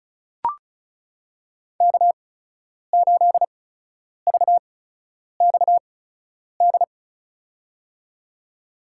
Morse code, heard as a keyed tone, about 34 wpm, spaced as for 10 wpm, 700 Hz. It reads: K8VXD
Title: USA Call Signs - 5 Characters - 8x Character Spacing - 35wpm